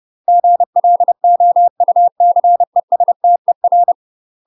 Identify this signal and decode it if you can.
Morse code, heard as a keyed tone, about 30 wpm, 700 Hz. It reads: GLOUCESTER
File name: UK MASTER QSOs TRAINING / 30WPM / UK MASTER QSOs 0050